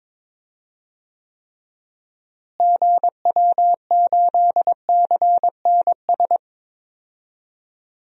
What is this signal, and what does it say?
Morse code, heard as a keyed tone, about 22 wpm, 700 Hz. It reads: GW8CNS
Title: DXCC-100 Call Signs - 6 Characters - Repeated 2x - 22wpm